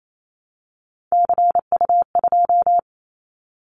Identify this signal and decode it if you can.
Morse code, heard as a keyed tone, about 28 wpm, 700 Hz. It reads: CU2
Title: Callsign Prefixes for Top 100 DXCC - 28wpm